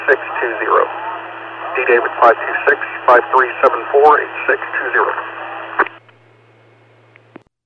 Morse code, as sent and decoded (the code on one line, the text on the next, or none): none